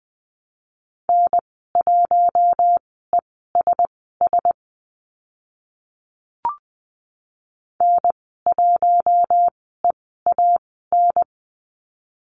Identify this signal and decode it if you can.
Morse code, heard as a keyed tone, about 20 wpm, spaced as for 14 wpm, 700 Hz. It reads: N1ESS N1EAN
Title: 042 Lesson 8g - Number 1 - Call Signs 20wpm